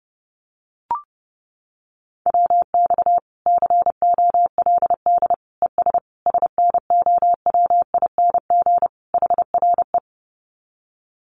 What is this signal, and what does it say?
Morse code, heard as a keyed tone, about 30 wpm, 700 Hz. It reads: WX COLD ES SNOWING HRE
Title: Common QSO Phrases 30wpm